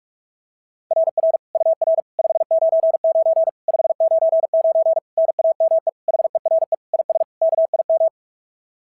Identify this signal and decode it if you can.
Morse code, heard as a keyed tone, about 46 wpm, 690 Hz. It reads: RR UR 599 599 NAME HERE IS KIM
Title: SOTA QSO Practice 45wpm